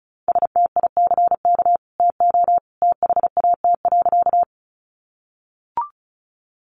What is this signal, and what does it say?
Morse code, heard as a keyed tone, about 34 wpm, 700 Hz. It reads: STICK TO THAT.
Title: Sentences from Top 1200 Words 35wpm